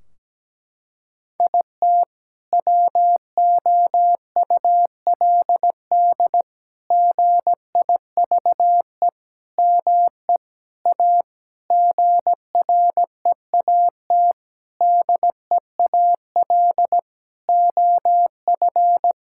Morse code, heard as a keyed tone, about 17 wpm, 700 Hz. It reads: IT WOULD GIVE ME A GREAT DEAL OF
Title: Sentences from Top 1600 Words 17wpm